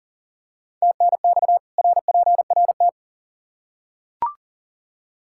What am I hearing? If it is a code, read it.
Morse code, heard as a keyed tone, about 39 wpm, 700 Hz. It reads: TNX RPRT
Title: K1USN SST QSO Practice 40wpm